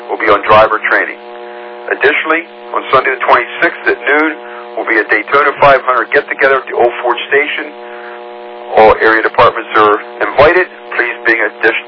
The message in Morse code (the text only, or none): none